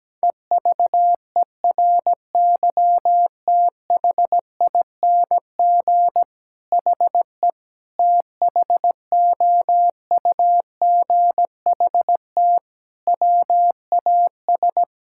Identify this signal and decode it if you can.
Morse code, heard as a keyed tone, about 17 wpm, 700 Hz. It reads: EVERYTHING HE THOUGHT WAS
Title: Sentences from Top 1000 Words - Review Part 6 - 17wpm